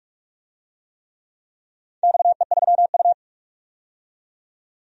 Morse code, heard as a keyed tone, about 46 wpm, 700 Hz. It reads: XE3U
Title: DXCC-100 Call Signs - 4 Characters - 45wpm